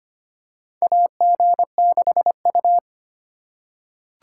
Morse code, heard as a keyed tone, about 25 wpm, 700 Hz. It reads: AG6U
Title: USA Call Signs - 4 Characters - 6x Character Spacing - 25wpm